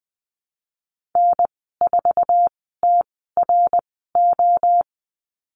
Morse code, heard as a keyed tone, about 20 wpm, spaced as for 14 wpm, 700 Hz. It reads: N4TRO
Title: 062 Lesson 11g - Letter h - Call Signs 20wpm